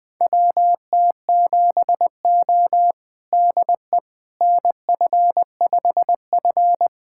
Morse code, heard as a keyed tone, about 21 wpm, 700 Hz. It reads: WT7O DE NF5F
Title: USA Callsign DE Callsign - 4 Characters - 20wpm